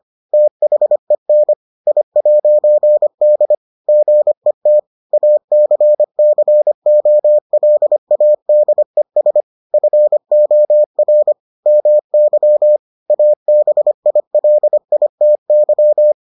Morse code, heard as a keyed tone, about 25 wpm, 600 Hz. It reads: THEN I'D GET ACCOLADES FOR MY ABILITY